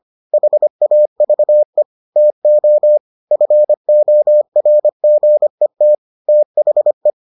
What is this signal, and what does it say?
Morse code, heard as a keyed tone, about 25 wpm, 600 Hz. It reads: HAVE TO FORGET THE